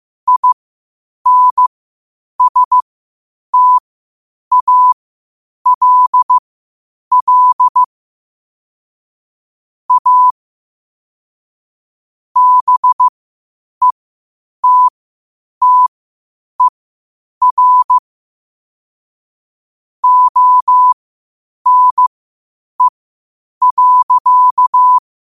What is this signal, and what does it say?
Morse code, heard as a keyed tone, about 15 wpm, spaced as for 8 wpm, 1000 Hz. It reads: INSTALL A BETTER ONE.
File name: qotdfiles / 8wpm